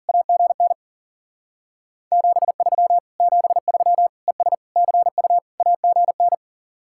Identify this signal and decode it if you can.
Morse code, heard as a keyed tone, about 52 wpm, 700 Hz. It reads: AGN 73 73 ES CU AGN